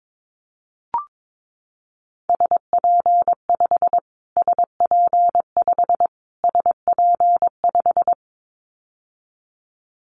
Morse code, heard as a keyed tone, about 22 wpm, 700 Hz. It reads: SP5 SP5 SP5
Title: Callsign Prefixes for Top 25 DXCC - Repeated 3x - 22wpm